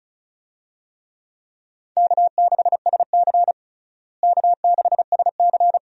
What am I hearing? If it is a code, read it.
Morse code, heard as a keyed tone, about 36 wpm, 700 Hz. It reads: K6SC K6SC
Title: USA Call Signs - 4 Characters - Repeated 2x - 35wpm